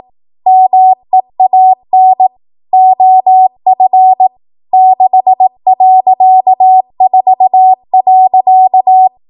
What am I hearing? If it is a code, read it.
Morse code, heard as a keyed tone, about 18 wpm, 750 Hz. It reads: MEAN OF 6.4.
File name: Morse / 18WPM